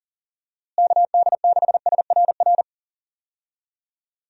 Morse code, heard as a keyed tone, about 40 wpm, 700 Hz. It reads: KD6SRR
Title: USA Call Signs - 6 Characters - Repeated 3x - 40wpm